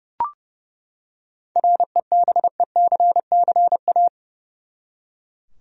Morse code, heard as a keyed tone, about 31 wpm, 700 Hz. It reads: REBECCA
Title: USA Common Names - 30wpm